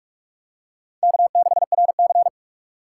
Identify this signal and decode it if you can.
Morse code, heard as a keyed tone, about 46 wpm, 700 Hz. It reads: K6RC
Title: K1USN SST QSO Practice - 45wpm